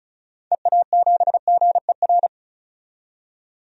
Morse code, heard as a keyed tone, about 34 wpm, 700 Hz. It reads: EA7GER
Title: DXCC-100 Call Signs - 6 Characters - 35wpm